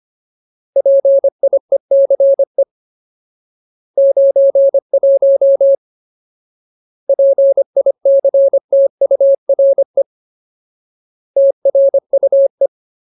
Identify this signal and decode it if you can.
Morse code, heard as a keyed tone, about 25 wpm, 550 Hz. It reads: PIECE 91 PICTURE TRUE
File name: Ultimate Word List / 25WPM Ultimate